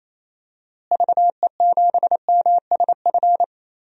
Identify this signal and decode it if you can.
Morse code, heard as a keyed tone, about 28 wpm, 700 Hz. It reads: VE7MSF
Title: DXCC-100 Call Signs - 6 Characters - Repeated 2x - 28wpm